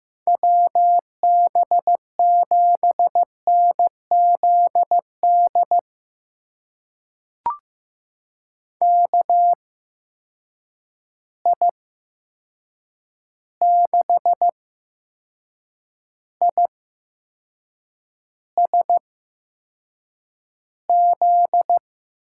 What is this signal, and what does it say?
Morse code, heard as a keyed tone, about 15 wpm, spaced as for 6 wpm, 700 Hz. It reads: WB7NZD KI6ISZ
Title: USA Call Signs - 6 Characters - 8x Character Spacing - 15wpm